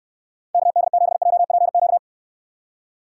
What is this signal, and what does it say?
Morse code, heard as a keyed tone, about 67 wpm, 700 Hz. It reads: KD7JPQ